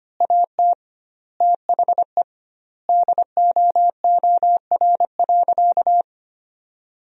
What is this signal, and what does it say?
Morse code, heard as a keyed tone, about 25 wpm, 700 Hz. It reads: AT THE DOOR.
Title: 2x Word-Spacing - Sentences from Top 400 Words - 25wpm